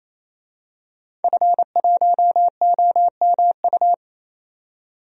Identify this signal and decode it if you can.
Morse code, heard as a keyed tone, about 28 wpm, 700 Hz. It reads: F1OMU
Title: DXCC-100 Call Signs - 5 Characters - 28wpm